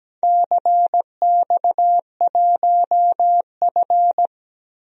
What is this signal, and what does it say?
Morse code, heard as a keyed tone, about 17 wpm, 700 Hz. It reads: CX1F
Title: DXCC-100 Call Signs - 4 Characters - 17wpm